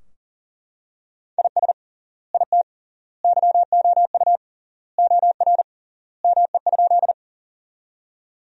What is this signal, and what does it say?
Morse code, heard as a keyed tone, about 40 wpm, 700 Hz. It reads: IS IT YOU OR ME?
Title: 3x Word-Spacing - Sentences from Top 200 Words - 40wpm